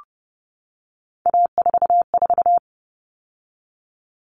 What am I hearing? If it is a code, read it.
Morse code, heard as a keyed tone, about 30 wpm, 700 Hz. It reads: A44
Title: All Cut Numbers - 3-Digit 30wpm